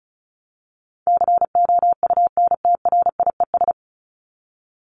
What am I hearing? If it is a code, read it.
Morse code, heard as a keyed tone, about 34 wpm, 700 Hz. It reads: COUNTRIES
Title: ES:S Suffix - Top 100 Words with Base Words 35wpm